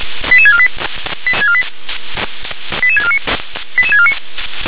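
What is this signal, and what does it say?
Five-tone selective call (ZVEI), heard as quick runs of five tones, tones of about 70 ms. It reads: 88648 88649 88649
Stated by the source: ZVEI five-tone call 88164 received on an SDR.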